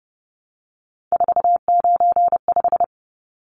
Morse code, heard as a keyed tone, about 30 wpm, 700 Hz. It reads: 495
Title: Numbers - 3-Digit 30wpm